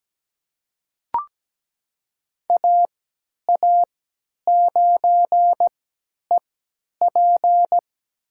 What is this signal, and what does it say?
Morse code, heard as a keyed tone, about 17 wpm, spaced as for 10 wpm, 700 Hz. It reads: AA9EP
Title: USA Call Signs - 5 Characters - 3x Character Spacing - 17wpm